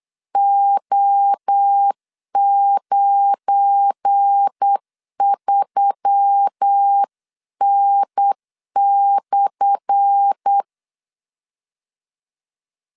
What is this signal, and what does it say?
Morse code, heard as a keyed tone, about 8 wpm, 800 Hz. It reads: O93N/
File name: LEZIONI / 10 SIGNS / 10A SLASH 8wpm 3452